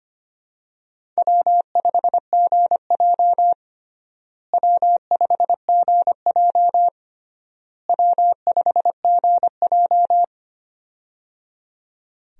Morse code, heard as a keyed tone, about 25 wpm, 700 Hz. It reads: W5GJ W5GJ W5GJ
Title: USA Call Signs - 4 Characters - Repeated 3x - 25wpm